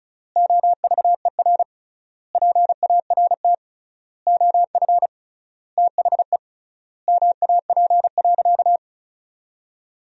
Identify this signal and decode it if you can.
Morse code, heard as a keyed tone, about 36 wpm, 700 Hz. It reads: OVER PART OF THE MAP.
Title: 3x Word-Spacing - Sentences from Top 400 Words - 35wpm